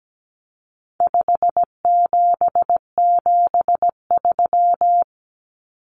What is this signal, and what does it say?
Morse code, heard as a keyed tone, about 17 wpm, 700 Hz. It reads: 5773
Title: Common Cut Numbers - 4-Digit 17wpm